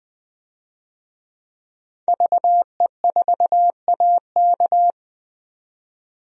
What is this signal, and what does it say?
Morse code, heard as a keyed tone, about 20 wpm, 700 Hz. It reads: VE4AK
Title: DXCC-100 Call Signs - 5 Characters - 20wpm